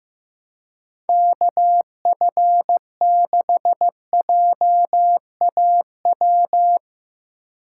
Morse code, heard as a keyed tone, about 18 wpm, 700 Hz. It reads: KF6JAW
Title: Speed Racing - USA 2x3 Call Signs 15wpm